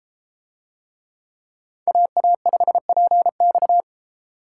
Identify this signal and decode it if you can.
Morse code, heard as a keyed tone, about 33 wpm, 700 Hz. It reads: AA5PX